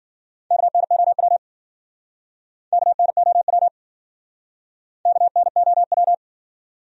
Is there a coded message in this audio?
Yes: Morse code from 0.5 to 6.1 s, heard as a keyed tone, about 52 wpm, 700 Hz. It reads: KNOW KNOW KNOW